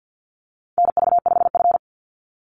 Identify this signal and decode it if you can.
Morse code, heard as a keyed tone, about 50 wpm, 700 Hz. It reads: NV5F
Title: Call Signs - 50wpm